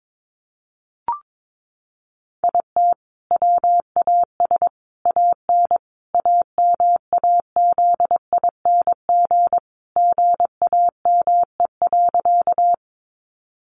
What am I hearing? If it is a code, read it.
Morse code, heard as a keyed tone, about 22 wpm, 700 Hz. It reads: IT WAS AN AMAZING GAME.
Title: Sentences from Top 1500 Words 22wpm